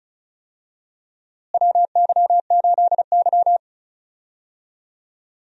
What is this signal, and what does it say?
Morse code, heard as a keyed tone, about 34 wpm, 700 Hz. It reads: WY8Y
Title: USA Call Signs - 4 Characters - Repeated 3x - 35wpm